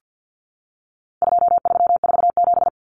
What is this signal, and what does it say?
Morse code, heard as a keyed tone, about 48 wpm, 700 Hz. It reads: 2347
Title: Numbers - 4-Digit 50wpm